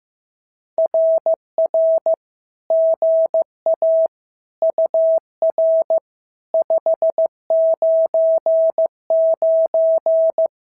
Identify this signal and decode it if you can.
Morse code, heard as a keyed tone, about 15 wpm, 690 Hz. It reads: RR GA UR 599
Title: SOTA QSO Practice 15wpm